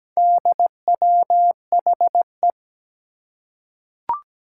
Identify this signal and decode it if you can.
Morse code, heard as a keyed tone, about 17 wpm, 700 Hz. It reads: DWHE
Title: Four Challenging Letter-Number Combinations 17wpm